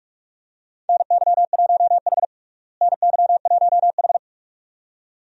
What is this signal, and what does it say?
Morse code, heard as a keyed tone, about 42 wpm, 700 Hz. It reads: NY1H NY1H
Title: USA Call Signs - 4 Characters - Repeated 2x - 45wpm